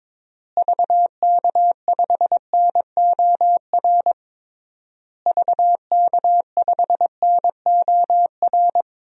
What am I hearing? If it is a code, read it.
Morse code, heard as a keyed tone, about 22 wpm, 700 Hz. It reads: VK5NOR VK5NOR
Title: DXCC-100 Call Signs - 6 Characters - Repeated 3x - 22wpm